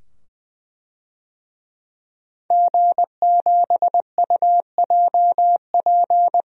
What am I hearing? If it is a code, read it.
Morse code, heard as a keyed tone, about 20 wpm, 700 Hz. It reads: G7UJP
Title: DXCC-100 Call Signs - 5 Characters - 20wpm